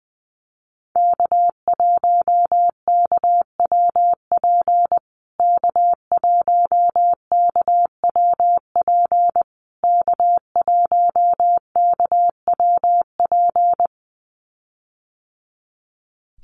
Morse code, heard as a keyed tone, about 20 wpm, 700 Hz. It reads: K1KWP K1KWP K1KWP
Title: USA Call Signs Repeated 3x 20wpm